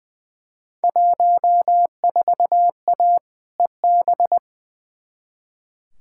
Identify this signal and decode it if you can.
Morse code, heard as a keyed tone, about 20 wpm, 700 Hz. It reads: 14A EB
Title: Field Day Exchanges Encore - 20wpm